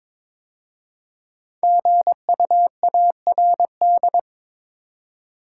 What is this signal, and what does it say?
Morse code, heard as a keyed tone, about 22 wpm, 700 Hz. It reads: GUARD
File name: 5 Letter Words - First 3 Letters Given - 22wpm